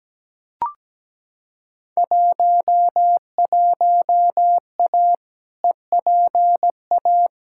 Morse code, heard as a keyed tone, about 17 wpm, 700 Hz. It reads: 11A EPA
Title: Field Day Exchanges Encore - 17wpm